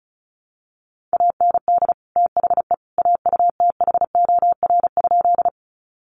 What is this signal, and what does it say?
Morse code, heard as a keyed tone, about 34 wpm, 700 Hz. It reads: AND THE AUTHOR?
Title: Sentences from Top 1200 Words 35wpm